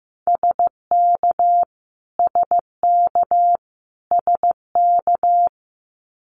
Morse code, heard as a keyed tone, about 15 wpm, 700 Hz. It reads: SK SK SK
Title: Canadian Provinces and Territories - Repeated 3x - 15wpm